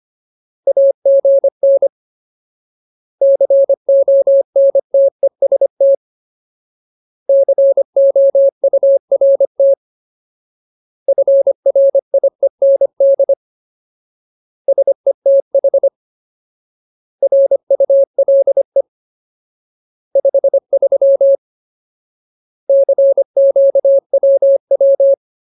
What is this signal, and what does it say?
Morse code, heard as a keyed tone, about 25 wpm, 550 Hz. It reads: AGN CONTEST COURT FRIEND SETH RULE 53 CQWW
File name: Ultimate Word List / 25WPM Ultimate